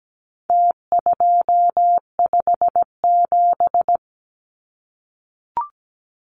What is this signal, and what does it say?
Morse code, heard as a keyed tone, about 17 wpm, 700 Hz. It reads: T257
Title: Numbers - Contest Style - 4-Digit 17wpm